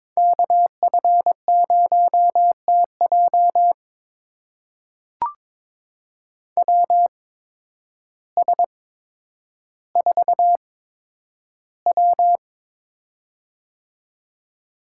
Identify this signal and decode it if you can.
Morse code, heard as a keyed tone, about 22 wpm, spaced as for 10 wpm, 700 Hz. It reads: KF0TJ WS4W
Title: USA Call Signs - 8x Character Spacing - 22wpm